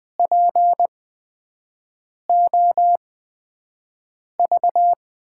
Morse code, heard as a keyed tone, about 20 wpm, spaced as for 5 wpm, 700 Hz. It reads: POV
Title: USA Call Signs - 6 Characters - 8x Character Spacing - 20wpm